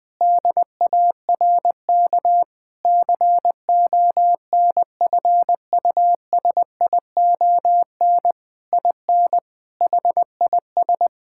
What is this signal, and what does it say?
Morse code, heard as a keyed tone, about 20 wpm, 700 Hz. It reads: DARK CONFUSION IN HIS
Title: Sentences from Top 1600 Words 20wpm